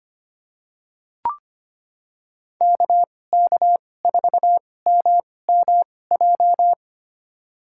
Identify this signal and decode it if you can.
Morse code, heard as a keyed tone, about 25 wpm, spaced as for 18 wpm, 700 Hz. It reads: KK4MMJ
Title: USA Call Signs - 6 Characters - 2x Character Spacing - 25wpm